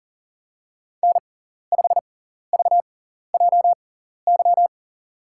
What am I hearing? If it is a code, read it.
Morse code, heard as a keyed tone, about 39 wpm, spaced as for 14 wpm, 700 Hz. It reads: N5VJY
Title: USA Call Signs - 6x Character Spacing - 40wpm